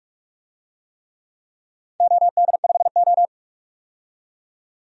Morse code, heard as a keyed tone, about 43 wpm, 700 Hz. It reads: OD5O
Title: DXCC-100 Call Signs - 4 Characters - 45wpm